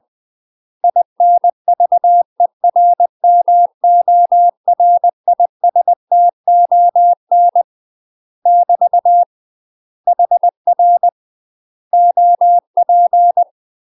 Morse code, heard as a keyed tone, about 20 wpm, 700 Hz. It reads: INVERMORISTON <BT> HR OP
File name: UK MASTER QSOs TRAINING/20WPM